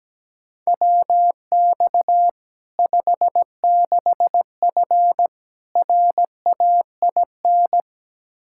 Morse code, heard as a keyed tone, about 17 wpm, 700 Hz. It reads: WX 56F RAIN
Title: Practice QSO - Exchange 2 - Info - 17wpm